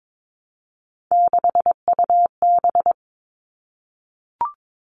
Morse code, heard as a keyed tone, about 22 wpm, 700 Hz. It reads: 6UB
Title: All Cut Numbers - 3-Digit 22wpm